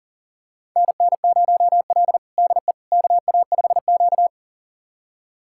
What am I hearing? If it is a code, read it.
Morse code, heard as a keyed tone, about 40 wpm, 700 Hz. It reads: NN0L DE KA5Q
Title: USA Callsign DE Callsign - 4 Characters - 40wpm